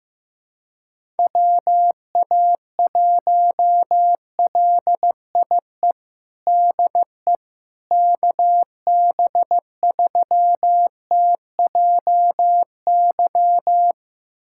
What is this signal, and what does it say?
Morse code, heard as a keyed tone, about 15 wpm, 700 Hz. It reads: WA1LIE DE KB3TJY
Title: USA Callsign DE Callsign - 6 Characters - 15wpm